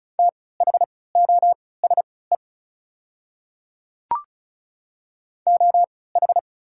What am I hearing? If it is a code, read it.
Morse code, heard as a keyed tone, about 36 wpm, spaced as for 20 wpm, 690 Hz. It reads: THOSE OH
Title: 3x Character Spacing - Top 500 Words - 35wpm